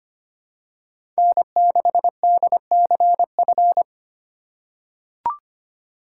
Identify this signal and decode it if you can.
Morse code, heard as a keyed tone, about 25 wpm, 700 Hz. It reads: N6DCF
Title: USA Call Signs - 5 Characters - 3x Character Spacing - 25wpm